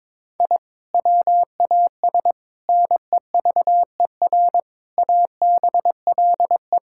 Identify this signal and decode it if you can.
Morse code, heard as a keyed tone, about 22 wpm, 700 Hz. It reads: I WAS NEVER ABLE
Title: Sentences from Top 1000 Words - Review Part 6 - 22wpm